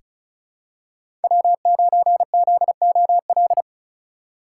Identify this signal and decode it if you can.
Morse code, heard as a keyed tone, about 34 wpm, 700 Hz. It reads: W9ZOL
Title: USA Call Signs - 5 Characters - 4x Character Spacing - 35wpm